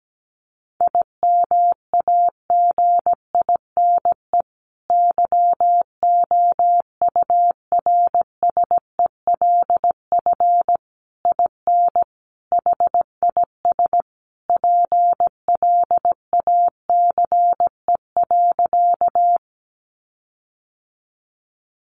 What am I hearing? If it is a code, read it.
Morse code, heard as a keyed tone, about 17 wpm, 700 Hz. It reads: IMAGINE YOURSELF IN HIS PLACE.
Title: Sentences from Top 1400 Words 17wpm